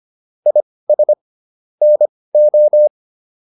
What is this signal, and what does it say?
Morse code, heard as a keyed tone, about 25 wpm, spaced as for 18 wpm, 600 Hz. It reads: IS NO